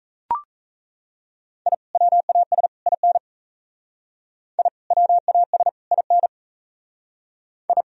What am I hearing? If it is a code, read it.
Morse code, heard as a keyed tone, about 36 wpm, 700 Hz. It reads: I WAS IN I WAS IN I